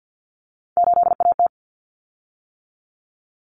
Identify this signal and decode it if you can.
Morse code, heard as a keyed tone, about 50 wpm, 700 Hz. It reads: 8AT
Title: Common Cut Numbers - 3-Digit 50wpm